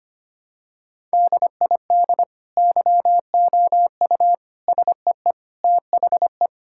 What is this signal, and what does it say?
Morse code, heard as a keyed tone, about 25 wpm, 700 Hz. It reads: DID YOU SEE THE
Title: Sentences from Top 1000 Words - Review Part 6 - 25wpm